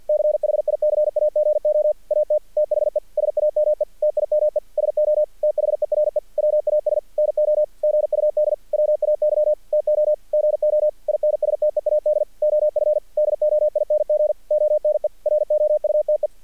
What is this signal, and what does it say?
Morse code, heard as a keyed tone, about 48 wpm, 600 Hz. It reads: CHICAGO AT THE SAME TIME SO THERE WAS NO GUD WAY TO GO INSTEAD OF DOING ONE ROUTE
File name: archived / CRSnet